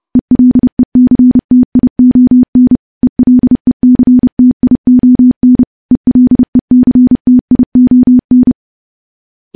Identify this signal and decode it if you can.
Morse code, heard as a keyed tone, about 30 wpm, 250 Hz. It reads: ELECTION ELECTION ELECTION
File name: morse / morse-training-30WPM